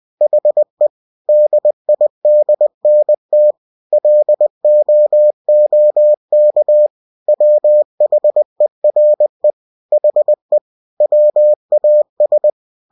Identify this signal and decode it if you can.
Morse code, heard as a keyed tone, about 20 wpm, 600 Hz. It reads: HE DIDNT LOOK WHERE HE WAS